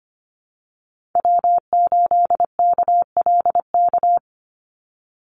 Morse code, heard as a keyed tone, about 25 wpm, 700 Hz. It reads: W8KLK
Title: USA Call Signs 25wpm